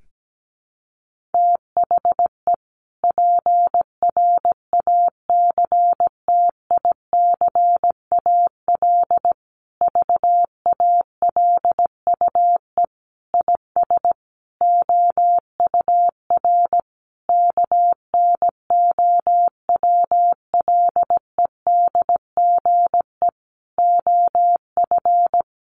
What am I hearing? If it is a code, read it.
Morse code, heard as a keyed tone, about 17 wpm, 700 Hz. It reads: THE PRACTICAL VALUE IS OUR KNOWLEDGE OF
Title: Sentences from Top 1400 Words 17wpm